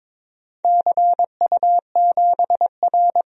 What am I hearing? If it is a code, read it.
Morse code, heard as a keyed tone, about 21 wpm, 700 Hz. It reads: CU7R
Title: DXCC-100 Call Signs - 4 Characters - Repeated 2x - 22wpm